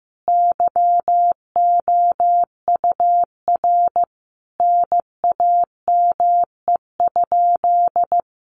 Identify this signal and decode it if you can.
Morse code, heard as a keyed tone, about 14 wpm, 700 Hz. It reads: YOUR NAME?
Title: Sentences from Top 200 Words 15wpm